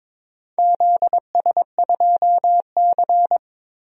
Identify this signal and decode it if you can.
Morse code, heard as a keyed tone, about 22 wpm, 700 Hz. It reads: ZS2C
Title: DXCC-100 Call Signs - 4 Characters - 22wpm